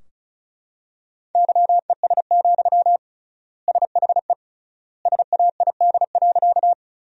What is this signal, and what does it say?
Morse code, heard as a keyed tone, about 34 wpm, 700 Hz. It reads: YES, SHE SAID.
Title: 3x Word-Spacing - Sentences from Top 500 Words - 35wpm